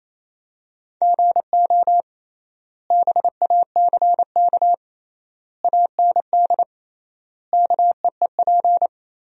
Morse code, heard as a keyed tone, about 28 wpm, 700 Hz. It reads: GO BACK AND KEEP
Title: 3x Word-Spacing - Sentences from Top 300 Words - 28wpm